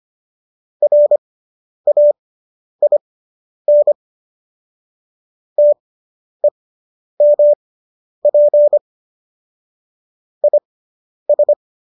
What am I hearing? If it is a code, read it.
Morse code, heard as a keyed tone, about 25 wpm, spaced as for 10 wpm, 590 Hz. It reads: RAIN TEMP IS